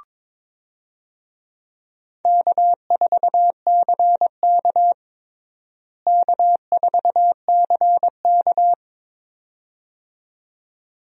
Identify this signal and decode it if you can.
Morse code, heard as a keyed tone, about 22 wpm, 700 Hz. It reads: K4CK K4CK
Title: USA Call Signs - 4 Characters - Repeated 2x - 22wpm